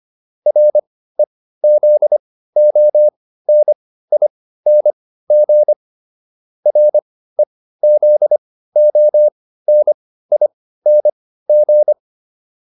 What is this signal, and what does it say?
Morse code, heard as a keyed tone, about 25 wpm, spaced as for 15 wpm, 600 Hz. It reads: REZONING REZONING